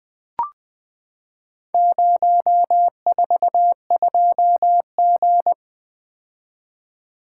Morse code, heard as a keyed tone, about 20 wpm, 700 Hz. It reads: 042G
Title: Four Letter-Number Combinations - 20wpm